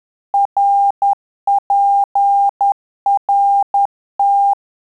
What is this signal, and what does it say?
Morse code, heard as a keyed tone, about 10 wpm, 800 Hz. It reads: RPRT
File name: Telegraphy-group / QSO CW 50 CPM